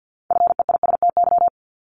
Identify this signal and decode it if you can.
Morse code, heard as a keyed tone, about 50 wpm, 700 Hz. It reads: FEISTY
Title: Multi-letter Phonemes Repeated 3x 50wpm